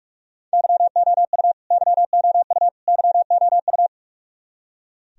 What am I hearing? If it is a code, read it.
Morse code, heard as a keyed tone, about 46 wpm, 700 Hz. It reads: YOU YOU YOU
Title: Top 100 Words Repeated 45wpm